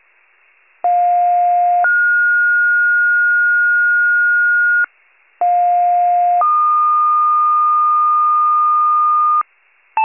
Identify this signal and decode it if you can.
Two-tone page, heard as a one-second tone, then a three-second tone, about 710 then 1430 Hz, about 710 then 1220 Hz, with noise. It is heard from 0.8 to 9.4 s.
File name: wcalert